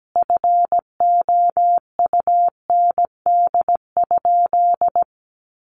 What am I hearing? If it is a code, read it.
Morse code, heard as a keyed tone, about 17 wpm, 700 Hz. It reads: FOUND?
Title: Sentences from Top 1400 Words 17wpm